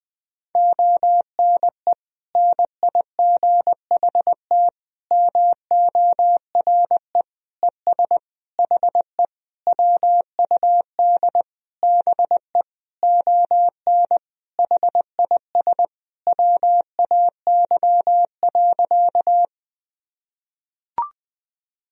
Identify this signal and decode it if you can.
Morse code, heard as a keyed tone, about 20 wpm, 700 Hz. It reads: ONE NIGHT MORE ES HE WUD BE ON HIS WAY.
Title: Sentences from Top 300 Words and Abbreviations 20wpm